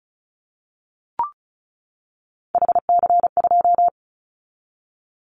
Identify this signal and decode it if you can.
Morse code, heard as a keyed tone, about 36 wpm, 700 Hz. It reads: HC2
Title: Callsign Prefixes for Top 100 DXCC - 35wpm